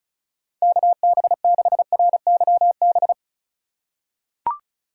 Morse code, heard as a keyed tone, about 34 wpm, 700 Hz. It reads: KB6RYB
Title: USA Call Signs - 2x Character Spacing - 35wpm